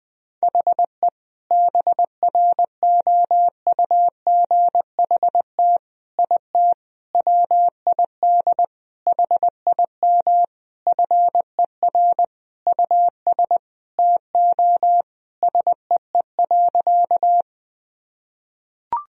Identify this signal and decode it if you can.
Morse code, heard as a keyed tone, about 21 wpm, 700 Hz. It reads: HE BROUGHT IT WID HIM FER US TO SEE.
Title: Sentences from Top 500 Words and Abbreviations 20wpm